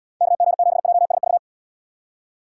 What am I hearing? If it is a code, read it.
Morse code, heard as a keyed tone, about 74 wpm, 700 Hz. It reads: KG7JSV